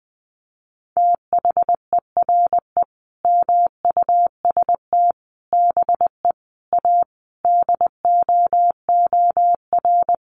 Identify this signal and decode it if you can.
Morse code, heard as a keyed tone, about 19 wpm, 700 Hz. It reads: THERE MUST BE A DOOR
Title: Sentences from Top 1400 Words 20wpm